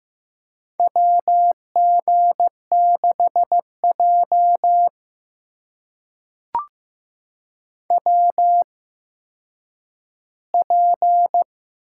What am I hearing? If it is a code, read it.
Morse code, heard as a keyed tone, about 15 wpm, spaced as for 9 wpm, 700 Hz. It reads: WG6J WP
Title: USA Call Signs - 8x Character Spacing - 15wpm